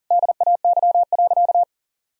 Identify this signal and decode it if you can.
Morse code, heard as a keyed tone, about 40 wpm, 700 Hz. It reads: DAY.
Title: Sentences from Top 1000 Words - Review Part 1 - 40wpm